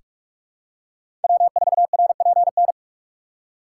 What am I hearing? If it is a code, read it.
Morse code, heard as a keyed tone, about 43 wpm, 700 Hz. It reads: W4RPN
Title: USA Call Signs - 5 Characters - 3x Character Spacing - 45wpm